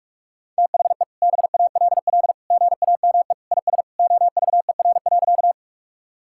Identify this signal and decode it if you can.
Morse code, heard as a keyed tone, about 46 wpm, 700 Hz. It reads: THE BALL GAME IS OVER.
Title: Sentences from Top 1000 Words - Review Part 6 - 45wpm